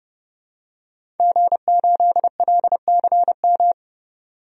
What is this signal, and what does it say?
Morse code, heard as a keyed tone, about 30 wpm, 700 Hz. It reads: G8LCM
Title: DXCC-100 Call Signs - 5 Characters - 30wpm